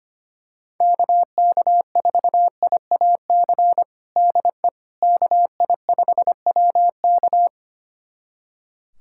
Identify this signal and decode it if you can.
Morse code, heard as a keyed tone, about 25 wpm, 700 Hz. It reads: KK4IAC DE KI5WK
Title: USA Callsign DE Callsign 25wpm